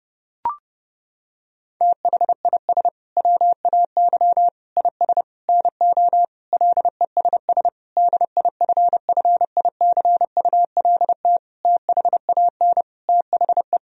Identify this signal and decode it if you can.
Morse code, heard as a keyed tone, about 32 wpm, 700 Hz. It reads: THIS WAY IS NO LESS DIFFICULT THAN THE
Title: Sentences from Top 1000 Words - Review Part 2 - 30wpm